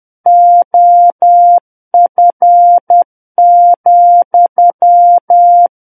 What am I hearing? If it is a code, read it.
Morse code, heard as a keyed tone, about 10 wpm, 700 Hz. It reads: OF,
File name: Book One / 01-Chapter 1 - 10 WPM